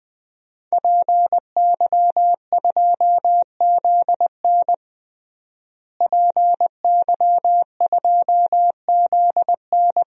Morse code, heard as a keyed tone, about 20 wpm, 700 Hz. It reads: PY2ZN PY2ZN
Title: DXCC-100 Call Signs - 5 Characters - Repeated 3x - 20wpm